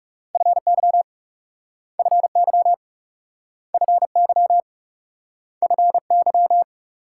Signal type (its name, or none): Morse code